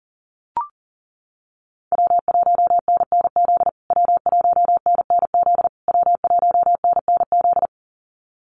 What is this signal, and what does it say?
Morse code, heard as a keyed tone, about 40 wpm, 700 Hz. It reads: W1NNZ W1NNZ W1NNZ
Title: USA Call Signs Repeated 3x 40wpm